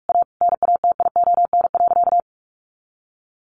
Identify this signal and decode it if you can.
Morse code, heard as a keyed tone, about 44 wpm, 700 Hz. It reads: A NATION.
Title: Sentences from Top 1400 Words 45wpm